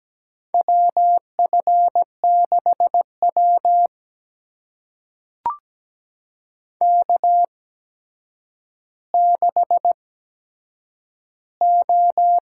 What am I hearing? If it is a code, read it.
Morse code, heard as a keyed tone, about 17 wpm, spaced as for 8 wpm, 700 Hz. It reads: WF6W K6O
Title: USA Call Signs - 4 Characters - 8x Character Spacing - 17wpm